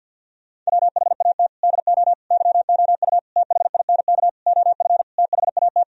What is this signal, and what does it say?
Morse code, heard as a keyed tone, about 50 wpm, 700 Hz. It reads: WHAT DO YOU THINK OF THAT
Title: Sentences from Top 1000 Words - Review Part 2 - 50wpm